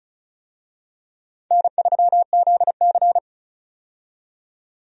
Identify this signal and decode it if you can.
Morse code, heard as a keyed tone, about 34 wpm, 700 Hz. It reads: N3ZC
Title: USA Call Signs - 4 Characters - 35wpm